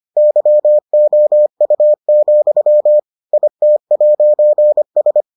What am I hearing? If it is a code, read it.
Morse code, heard as a keyed tone, about 25 wpm, 600 Hz. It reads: YOU, IT'S